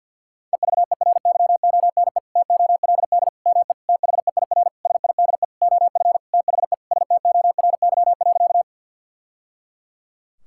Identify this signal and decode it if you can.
Morse code, heard as a keyed tone, about 52 wpm, 700 Hz. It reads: EVERYONE TOLD ME THEIR SIDE OF THE STORY.
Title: Sentences from Top 1000 Words - Review Part 5 - 50wpm